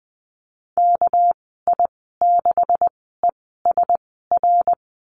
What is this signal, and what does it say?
Morse code, heard as a keyed tone, about 20 wpm, spaced as for 15 wpm, 700 Hz. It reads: KI6ESR
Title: 200 Lesson 32g - Letter k - Call Signs 20wpm